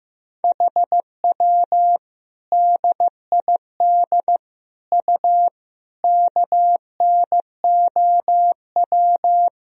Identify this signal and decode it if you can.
Morse code, heard as a keyed tone, about 15 wpm, 700 Hz. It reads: HW DID U KNOW
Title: Sentences from Top 100 Words and Abbreviations 15wpm